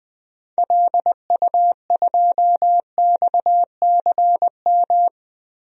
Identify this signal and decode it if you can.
Morse code, heard as a keyed tone, about 20 wpm, 700 Hz. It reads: LU2XCM
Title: DXCC-100 Call Signs - 6 Characters - 20wpm